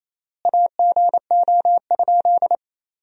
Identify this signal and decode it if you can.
Morse code, heard as a keyed tone, about 28 wpm, 700 Hz. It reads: AGO?
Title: 3x Word-Spacing - Sentences from Top 500 Words - 28wpm